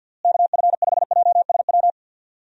Morse code, heard as a keyed tone, about 52 wpm, 700 Hz. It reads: KF5JSW
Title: USA Call Signs - 6 Characters - Repeated 3x - 50wpm